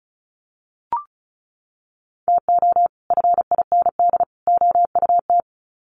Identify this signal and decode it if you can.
Morse code, heard as a keyed tone, about 34 wpm, 700 Hz. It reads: TO FIND OUT
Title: Sets of 3 Words 35wpm